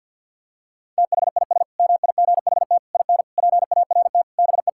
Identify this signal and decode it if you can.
Morse code, heard as a keyed tone, about 51 wpm, 700 Hz. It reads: THIS MIGHT IN PART BE
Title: Sentences from Top 1000 Words - Review Part 5 - 50wpm